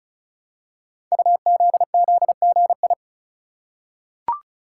Morse code, heard as a keyed tone, about 36 wpm, 700 Hz. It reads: UZZGI
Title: Code Groups - 5 Letters - 35wpm